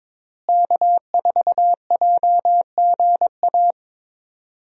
Morse code, heard as a keyed tone, about 22 wpm, 700 Hz. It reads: K4JGA
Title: USA Call Signs - 5 Characters - Repeated 3x - 22wpm